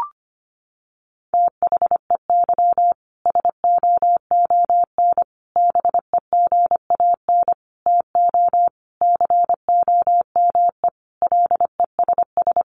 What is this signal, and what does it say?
Morse code, heard as a keyed tone, about 25 wpm, 700 Hz. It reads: THEY SOON BEGAN TO COME LESS
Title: Sentences from Top 1300 Words 25wpm